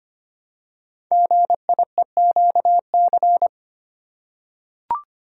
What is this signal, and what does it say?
Morse code, heard as a keyed tone, about 25 wpm, 700 Hz. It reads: GIEQC
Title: Code Groups - 5 Letter:Numbers - 25wpm